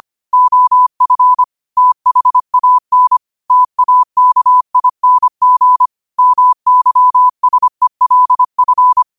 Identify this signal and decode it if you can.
Morse code, heard as a keyed tone, about 25 wpm, 1000 Hz. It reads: OF THAN TAKING MYSELF
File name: qotdfiles / 25wpm